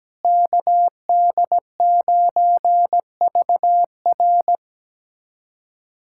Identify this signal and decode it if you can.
Morse code, heard as a keyed tone, about 17 wpm, 700 Hz. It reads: KD9VR
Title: USA Callsign DE Callsign - 5 Characters - 17wpm